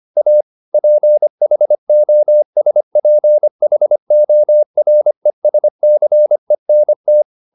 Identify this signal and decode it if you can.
Morse code, heard as a keyed tone, about 25 wpm, 600 Hz. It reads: A PHOSPHORESCENT
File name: the old man and the sea part 1 25wpm